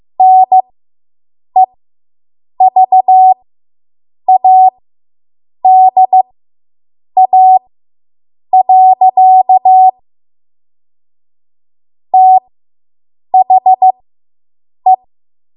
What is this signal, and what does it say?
Morse code, heard as a keyed tone, about 15 wpm, spaced as for 7 wpm, 750 Hz. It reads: NEVADA. THE